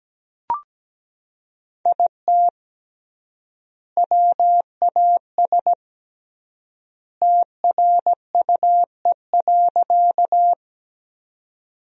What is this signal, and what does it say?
Morse code, heard as a keyed tone, about 17 wpm, 700 Hz. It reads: IT WAS TRUE.
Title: 3x Word-Spacing - Sentences from Top 400 Words - 17wpm